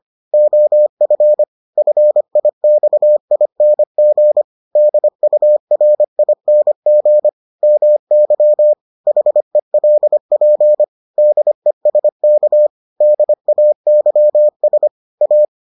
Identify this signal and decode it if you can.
Morse code, heard as a keyed tone, about 25 wpm, 600 Hz. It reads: OF FIXING DURING MY HELP DESK DAYS A